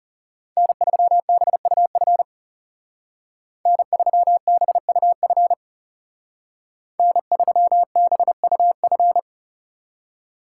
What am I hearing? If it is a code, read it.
Morse code, heard as a keyed tone, about 36 wpm, 700 Hz. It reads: N3BUF N3BUF N3BUF
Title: Speed Racing - USA Call Signs 30wpm